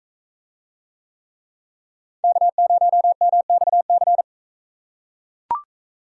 Morse code, heard as a keyed tone, about 37 wpm, 700 Hz. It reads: K0MXC